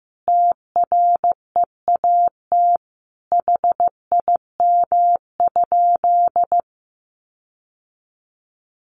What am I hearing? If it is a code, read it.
Morse code, heard as a keyed tone, about 15 wpm, 700 Hz. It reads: TREAT HIM?
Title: Sentences from Top 1300 Words 15wpm